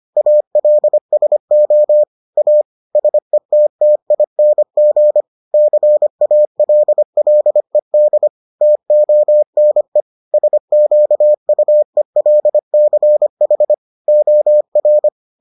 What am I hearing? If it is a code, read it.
Morse code, heard as a keyed tone, about 25 wpm, 600 Hz. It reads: ALSO A SETTING CALLED TONE SQUELCH OR